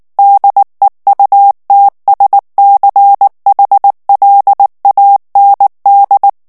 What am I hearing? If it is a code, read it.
Morse code, heard as a keyed tone, about 19 wpm, 800 Hz. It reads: DEUTSCHLAND
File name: morse mp3 / ALLEMAND texte clair 2 vitesse 18 mots minute